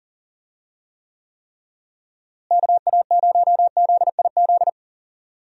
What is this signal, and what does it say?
Morse code, heard as a keyed tone, about 40 wpm, 700 Hz. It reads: KA0ZIZ
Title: USA Call Signs - 6 Characters - Repeated 3x - 40wpm